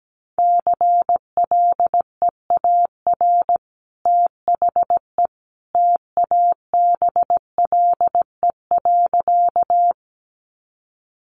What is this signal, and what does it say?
Morse code, heard as a keyed tone, about 17 wpm, 700 Hz. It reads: CLEAR THE TABLE.
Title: Sentences from Top 500 Words 17wpm